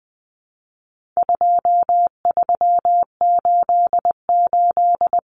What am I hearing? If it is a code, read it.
Morse code, heard as a keyed tone, about 20 wpm, 700 Hz. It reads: 2388
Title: Numbers - 4-Digit 20wpm